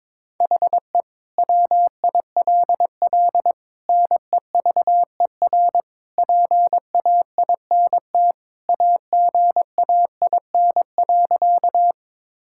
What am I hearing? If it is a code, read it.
Morse code, heard as a keyed tone, about 22 wpm, 700 Hz. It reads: HE WILL NEVER PAINT AGAIN.
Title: Sentences from Top 1000 Words - Review Part 6 - 22wpm